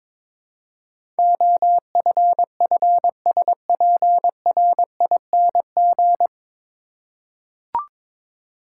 Morse code, heard as a keyed tone, about 22 wpm, 700 Hz. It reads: OFFSPRING
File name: 9 Letter Words - First 3 Letters Given - 22wpm